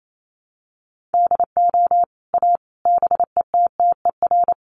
Morse code, heard as a keyed tone, about 28 wpm, 700 Hz. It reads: DO A BETTER
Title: Sets of 5 Words Encore2 28wpm